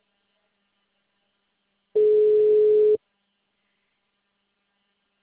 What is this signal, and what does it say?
Ringback tone (Europe, 425 Hz).
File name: out- -2007- - - .12268